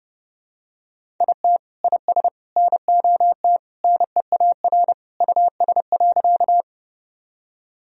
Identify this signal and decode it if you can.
Morse code, heard as a keyed tone, about 32 wpm, 700 Hz. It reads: IT IS NOT NEAR US.
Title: Sentences from Top 1000 Words - Review Part 1 - 30wpm